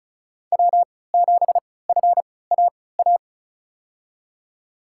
Morse code, heard as a keyed tone, about 36 wpm, spaced as for 20 wpm, 700 Hz. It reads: W7FAA
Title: USA Call Signs - 5 Characters - 3x Character Spacing - 35wpm